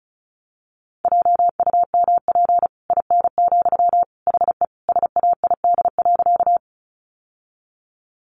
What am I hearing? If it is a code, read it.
Morse code, heard as a keyed tone, about 34 wpm, 700 Hz. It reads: JUMP IN, HE SAID.
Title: Sentences from Top 1300 Words 35wpm